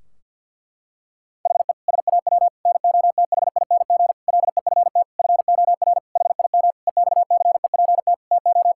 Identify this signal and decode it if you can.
Morse code, heard as a keyed tone, about 50 wpm, 700 Hz. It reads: HE SAW NOTHING LEFT FOR HIM EXCEPT TO
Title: Sentences from Top 1600 Words 50wpm